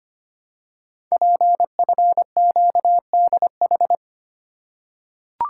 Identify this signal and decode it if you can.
Morse code, heard as a keyed tone, about 25 wpm, 700 Hz. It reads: PFQDH
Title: Code Groups - 5 Letters - 25wpm